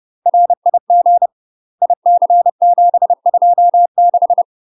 Morse code, heard as a keyed tone, about 30 wpm, 700 Hz. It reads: RIG IC726
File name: UK MASTER QSOs TRAINING / 30WPM / UK MASTER QSOs 0014